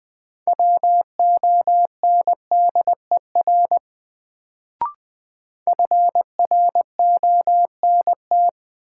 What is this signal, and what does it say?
Morse code, heard as a keyed tone, about 20 wpm, 700 Hz. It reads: WONDER FRONT
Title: Sentences from Top 500 Words - New Word Drill - 20wpm